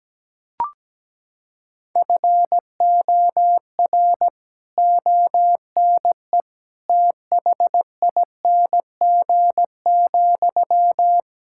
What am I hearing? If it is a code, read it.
Morse code, heard as a keyed tone, about 17 wpm, 700 Hz. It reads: FOR ONE THING,
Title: Sentences from Top 1000 Words - Review Part 6 - 17wpm